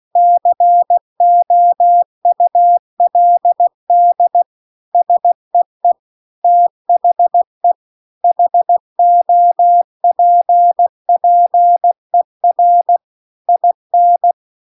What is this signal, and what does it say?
Morse code, heard as a keyed tone, about 16 wpm, 700 Hz. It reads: COULD SEE THE HOPPER IN
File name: CW-The-Big-Twoheaded-River-2-16WPM